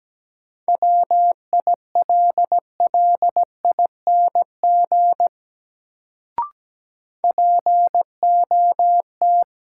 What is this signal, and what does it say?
Morse code, heard as a keyed tone, about 17 wpm, 700 Hz. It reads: WILLING POT
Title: Sentences from Top 1000 Words - New Word Drill - 17wpm